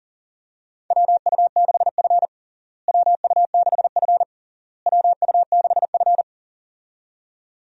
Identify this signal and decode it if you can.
Morse code, heard as a keyed tone, about 40 wpm, 700 Hz. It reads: WU6F WU6F WU6F
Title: USA Call Signs - 4 Characters - Repeated 3x - 40wpm